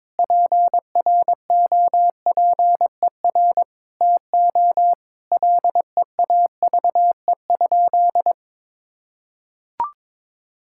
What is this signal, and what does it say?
Morse code, heard as a keyed tone, about 21 wpm, 700 Hz. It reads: PROPER TO LEAVE?
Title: Sentences from Top 1000 Words - Review Part 5 - 22wpm